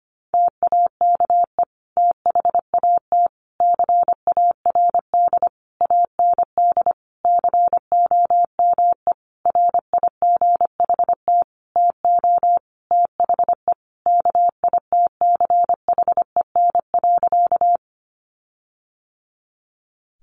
Morse code, heard as a keyed tone, about 25 wpm, 700 Hz. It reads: TAKE THAT CARD AND COME RIGHT TO THE KITCHEN.
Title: Sentences from Top 1400 Words 25wpm